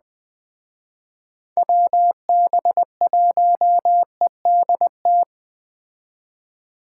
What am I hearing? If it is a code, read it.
Morse code, heard as a keyed tone, about 20 wpm, 700 Hz. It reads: WB1EDT